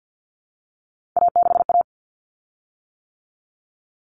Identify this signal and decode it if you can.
Morse code, heard as a keyed tone, about 48 wpm, 700 Hz. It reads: A6A
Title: Common Cut Numbers - 3-Digit 50wpm